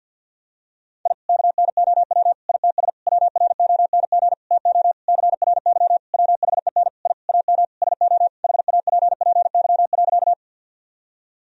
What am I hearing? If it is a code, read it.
Morse code, heard as a keyed tone, about 51 wpm, 700 Hz. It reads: I KNOW ITS WRONG TO CRY WHEN I AM SO HAPPY.
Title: Sentences from Top 1000 Words - Review Part 2 - 50wpm